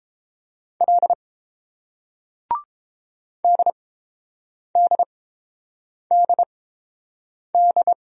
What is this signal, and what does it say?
Morse code, heard as a keyed tone, about 30 wpm, spaced as for 7 wpm, 690 Hz. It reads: L DDDD